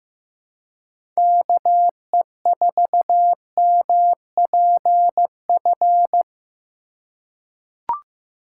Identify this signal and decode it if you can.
Morse code, heard as a keyed tone, about 15 wpm, 700 Hz. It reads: KE4MPF
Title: USA Call Signs - 6 Characters - 3x Character Spacing - 15wpm